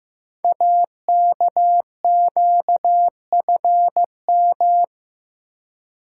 Morse code, heard as a keyed tone, about 15 wpm, 700 Hz. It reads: AKQFM
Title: Code Groups - 5 Letter:Numbers - 15wpm